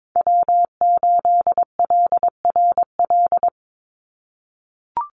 Morse code, heard as a keyed tone, about 22 wpm, 700 Hz. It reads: W8LRL
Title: USA Call Signs 22wpm